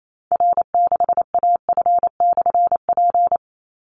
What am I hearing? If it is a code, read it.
Morse code, heard as a keyed tone, about 28 wpm, 700 Hz. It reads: R6AF/P
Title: Call Signs Repeated 3x - 28wpm